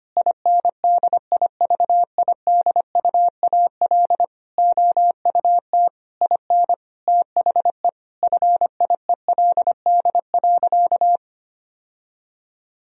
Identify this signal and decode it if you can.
Morse code, heard as a keyed tone, about 25 wpm, 700 Hz. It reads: INDIVIDUAL OUT IN THE FIELD.
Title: Sentences from Top 1000 Words - Review Part 4 - 25wpm